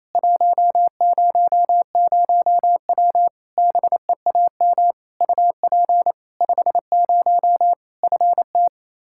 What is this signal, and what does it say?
Morse code, heard as a keyed tone, about 28 wpm, 700 Hz. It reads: 100W BEAM UP 50 FT
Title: Practice QSO - Exchange 2 - Info - 28wpm